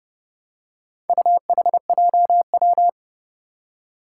Morse code, heard as a keyed tone, about 30 wpm, 700 Hz. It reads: UHJW
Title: Four Challenging Letter-Number Combinations 30wpm